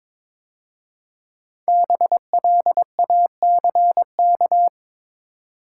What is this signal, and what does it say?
Morse code, heard as a keyed tone, about 22 wpm, 700 Hz. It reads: BLACK
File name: 5 Letter Words - First 3 Letters Given - 22wpm